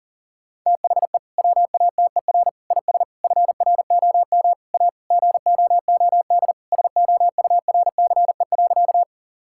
Morse code, heard as a keyed tone, about 40 wpm, 700 Hz. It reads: THE WATER IS FROM A GOOD SOURCE.
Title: Sentences from Top 1000 Words - Review Part 5 - 40wpm